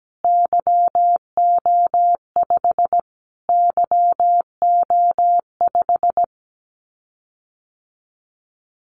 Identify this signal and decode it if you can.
Morse code, heard as a keyed tone, about 17 wpm, 700 Hz. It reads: YO5 YO5
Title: Callsign Prefixes for Top 25 DXCC - Repeated 3x - 17wpm